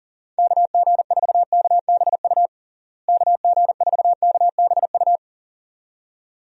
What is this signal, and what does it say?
Morse code, heard as a keyed tone, about 40 wpm, 700 Hz. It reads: KG4KBU KG4KBU
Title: USA Call Signs - 6 Characters - Repeated 2x - 40wpm